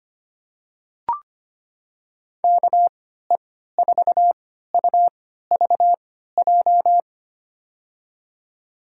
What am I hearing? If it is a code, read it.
Morse code, heard as a keyed tone, about 25 wpm, spaced as for 14 wpm, 700 Hz. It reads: KE4UVJ
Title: USA Call Signs - 6 Characters - 3x Character Spacing - 25wpm